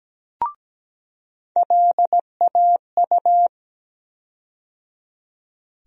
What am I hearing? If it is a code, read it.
Morse code, heard as a keyed tone, about 17 wpm, 700 Hz. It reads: LAU
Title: USA Call Sign Postfixes - 3 Letters 17wpm